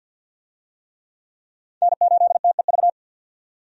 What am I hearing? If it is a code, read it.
Morse code, heard as a keyed tone, about 52 wpm, 700 Hz. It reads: N8TEV
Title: USA Call Signs - 5 Characters - 50wpm